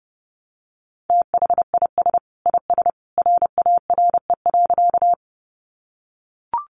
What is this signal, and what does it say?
Morse code, heard as a keyed tone, about 30 wpm, 700 Hz. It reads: THIS IS RARE.
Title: Sentences from Top 1100 Words 30wpm